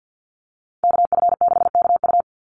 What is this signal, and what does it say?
Morse code, heard as a keyed tone, about 49 wpm, 700 Hz. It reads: KF6KU
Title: USA Call Signs 50wpm